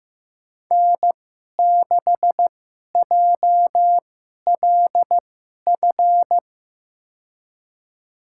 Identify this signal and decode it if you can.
Morse code, heard as a keyed tone, about 15 wpm, spaced as for 11 wpm, 700 Hz. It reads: N6JLF
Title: USA Call Signs - 5 Characters - 2x Character Spacing - 15wpm